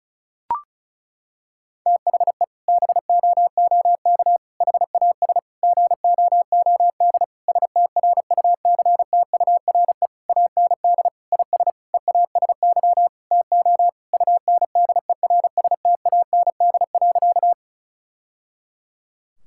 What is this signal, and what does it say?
Morse code, heard as a keyed tone, about 36 wpm, 700 Hz. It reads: THE BOOK HAS GOOD STRUCTURE AND IS EASY TO UNDERSTAND.
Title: Sentences from Top 1000 Words - Review Part 6 - 35wpm